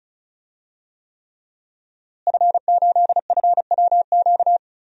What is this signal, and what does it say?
Morse code, heard as a keyed tone, about 34 wpm, 700 Hz. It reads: F8FWQ
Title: DXCC-100 Call Signs - 5 Characters - Repeated 3x - 35wpm